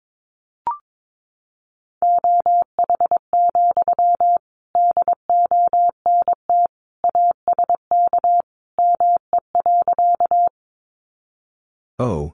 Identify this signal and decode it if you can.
Morse code, heard as a keyed tone, about 22 wpm, 700 Hz. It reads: OH, DONT ASK ME.
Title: Sentences from Top 500 Words 22wpm